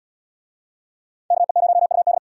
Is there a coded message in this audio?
Yes: Morse code from 1.3 to 2.2 s, heard as a keyed tone, about 68 wpm, 700 Hz. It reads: KE0UD